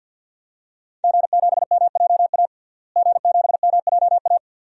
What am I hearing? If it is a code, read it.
Morse code, heard as a keyed tone, about 53 wpm, 700 Hz. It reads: G7MJA G7MJA
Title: DXCC-100 Call Signs - 5 Characters - Repeated 2x - 50wpm